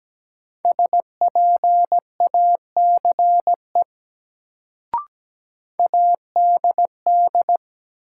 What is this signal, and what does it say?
Morse code, heard as a keyed tone, about 17 wpm, 700 Hz. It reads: SPACE ADD
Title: Sentences from Top 1500 Words - New Word Drill Alt - 17wpm